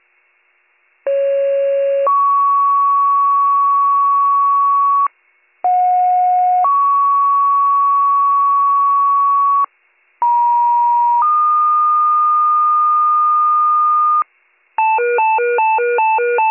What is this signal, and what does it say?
Two-tone page, heard as a one-second tone, then a three-second tone, about 550 then 1120 Hz, about 730 then 1120 Hz, about 950 then 1220 Hz. It is heard from 1.1 to 14.2 s.